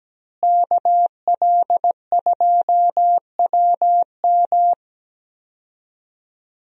Morse code, heard as a keyed tone, about 17 wpm, 700 Hz. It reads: KL2WM
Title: USA Call Signs - 5 Characters - Repeated 3x - 17wpm